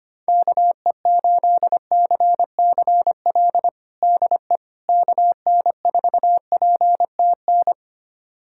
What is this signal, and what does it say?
Morse code, heard as a keyed tone, about 25 wpm, 700 Hz. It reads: KE8CCL DE KN4PTN
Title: USA Callsign DE Callsign - 6 Characters - 25wpm